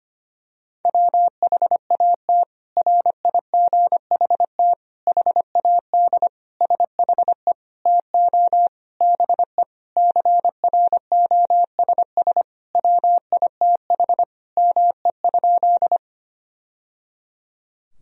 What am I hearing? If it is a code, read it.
Morse code, heard as a keyed tone, about 25 wpm, 700 Hz. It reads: WHAT RIGHT HAD SHE TO BE CROSS WITH ME?
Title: Sentences from Top 1000 Words - Review Part 1 - 25wpm